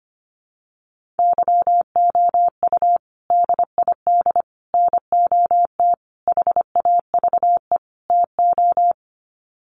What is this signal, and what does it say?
Morse code, heard as a keyed tone, about 25 wpm, 700 Hz. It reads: YOU DID NOT HAVE TO
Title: Sets of 5 Words Encore 25wpm